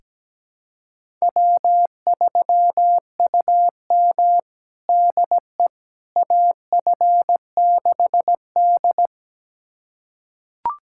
Morse code, heard as a keyed tone, about 17 wpm, 700 Hz. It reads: W3UM DE AF6D
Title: USA Callsign DE Callsign - 4 Characters - 17wpm